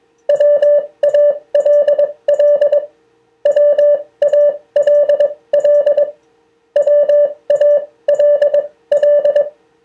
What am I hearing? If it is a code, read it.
Morse code, heard as a keyed tone, about 22 wpm, spaced as for 18 wpm, 580 Hz. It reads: WALL WALL WALL
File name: 2of4-words-1